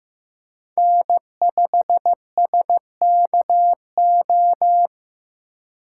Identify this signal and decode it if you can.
Morse code, heard as a keyed tone, about 18 wpm, 700 Hz. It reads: N5SKO
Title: Speed Racing - USA 1x3 Call Signs 15wpm